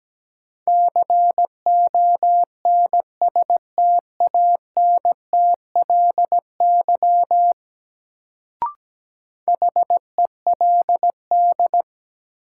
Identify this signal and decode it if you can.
Morse code, heard as a keyed tone, about 17 wpm, 700 Hz. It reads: CONSTANTLY HELD
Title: Sentences from Top 1600 Words - New Word Drill Alt - 17wpm